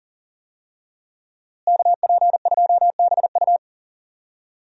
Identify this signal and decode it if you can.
Morse code, heard as a keyed tone, about 39 wpm, 700 Hz. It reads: KP2BU
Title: USA Call Signs - 5 Characters - Repeated 3x - 40wpm